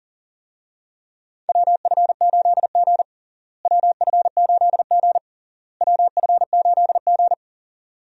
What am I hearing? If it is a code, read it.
Morse code, heard as a keyed tone, about 40 wpm, 700 Hz. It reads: WF8G WF8G WF8G
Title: USA Call Signs - 4 Characters - Repeated 3x - 40wpm